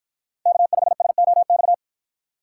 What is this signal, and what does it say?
Morse code, heard as a keyed tone, about 53 wpm, 700 Hz. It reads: K5SOX